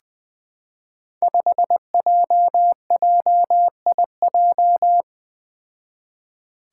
Morse code, heard as a keyed tone, about 20 wpm, 700 Hz. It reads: 5JJIJ
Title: Five Challenging Letter-Number Combinations 20wpm